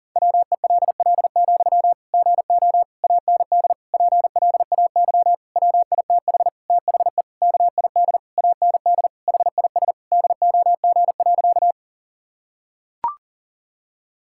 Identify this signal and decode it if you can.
Morse code, heard as a keyed tone, about 40 wpm, 700 Hz. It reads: WELL, GO AND PLAY WITH THE KID AND HIS DOG.
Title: Sentences from Top 1600 Words 40wpm